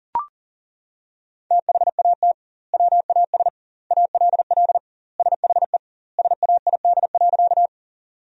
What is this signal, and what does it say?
Morse code, heard as a keyed tone, about 40 wpm, 700 Hz. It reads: THAT WAS ALL SHE SAID.
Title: 2x Word-Spacing - Sentences from Top 100 Words - 40wpm